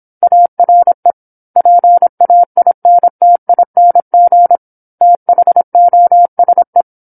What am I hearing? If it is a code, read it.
Morse code, heard as a keyed tone, about 26 wpm, 700 Hz. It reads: ARE PAINTING THOSE
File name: 08-Chapter-08 26 WPM